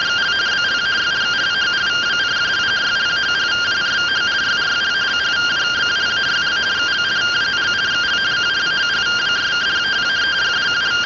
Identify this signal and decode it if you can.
Two-tone teleprinter signal, about 170 Hz shift.
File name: findings / CT 205 ax mips, iDose 3 0006.gen-39 rtty rx